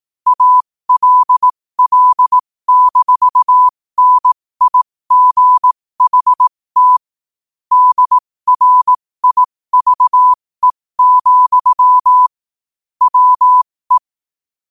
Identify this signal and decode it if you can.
Morse code, heard as a keyed tone, about 18 wpm, spaced as for 15 wpm, 1000 Hz. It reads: ALL-NIGHT DRIVE, WE
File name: qotdfiles / 15wpm